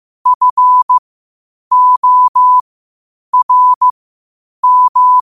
Morse code, heard as a keyed tone, about 15 wpm, spaced as for 8 wpm, 1000 Hz. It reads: FORM
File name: qotdfiles / 8wpm